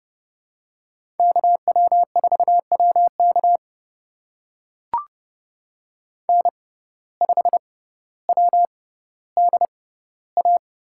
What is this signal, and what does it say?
Morse code, heard as a keyed tone, about 31 wpm, spaced as for 16 wpm, 700 Hz. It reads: KW4WK N5WDA
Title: USA Call Signs - 6x Character Spacing - 30wpm